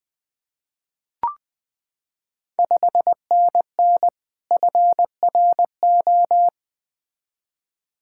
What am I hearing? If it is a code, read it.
Morse code, heard as a keyed tone, about 20 wpm, 700 Hz. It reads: 5NN FRO
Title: Ontario QSO Party Exchange 20wpm